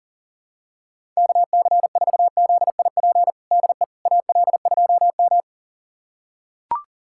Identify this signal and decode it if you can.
Morse code, heard as a keyed tone, about 37 wpm, 700 Hz. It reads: KC4ZIP DE AL2M
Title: USA Callsign DE Callsign 40wpm